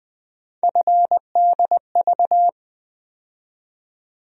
Morse code, heard as a keyed tone, about 20 wpm, 700 Hz. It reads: FDV
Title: USA Call Sign Postfixes - 3 Letters 20wpm